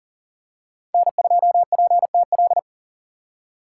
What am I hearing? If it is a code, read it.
Morse code, heard as a keyed tone, about 40 wpm, 700 Hz. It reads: N2PTL
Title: USA Call Signs - 2x Character Spacing - 40wpm